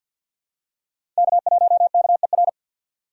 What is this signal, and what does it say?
Morse code, heard as a keyed tone, about 53 wpm, 700 Hz. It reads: K1KER
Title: USA Call Signs - 2x Character Spacing - 50wpm